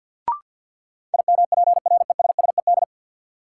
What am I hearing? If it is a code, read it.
Morse code, heard as a keyed tone, about 53 wpm, 700 Hz. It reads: IMPRESSED
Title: Sentences from Top 1600 Words - New Word Drill Alt - 50wpm